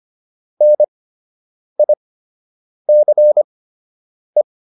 Morse code, heard as a keyed tone, about 25 wpm, spaced as for 8 wpm, 600 Hz. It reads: NICE